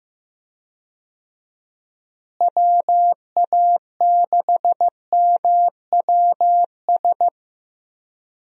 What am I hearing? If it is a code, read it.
Morse code, heard as a keyed tone, about 15 wpm, 700 Hz. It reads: WA6MWS
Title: USA Call Signs - 6 Characters - Repeated 3x - 15wpm